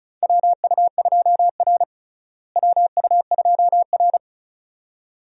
Morse code, heard as a keyed tone, about 34 wpm, 700 Hz. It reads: WU2R WU2R
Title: USA Call Signs - 4 Characters - Repeated 2x - 35wpm